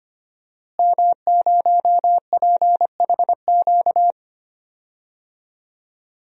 Morse code, heard as a keyed tone, about 25 wpm, 700 Hz. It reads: M0PHQ
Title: DXCC-100 Call Signs - 5 Characters - Repeated 3x - 25wpm